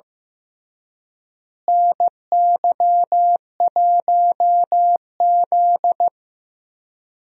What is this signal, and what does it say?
Morse code, heard as a keyed tone, about 16 wpm, 700 Hz. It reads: NY1Z
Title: USA Call Signs - 4 Characters - Repeated 3x - 15wpm